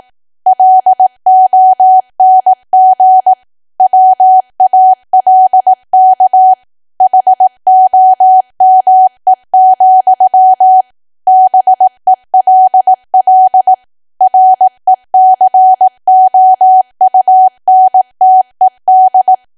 Morse code, heard as a keyed tone, about 18 wpm, 750 Hz. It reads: LONG WALK HOME, BELL RECOUNTED
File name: Archive / 18 WPM / 18WPM